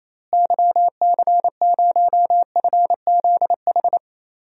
Morse code, heard as a keyed tone, about 28 wpm, 700 Hz. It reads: YC0FZH
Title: DXCC-100 Call Signs - 6 Characters - Repeated 2x - 28wpm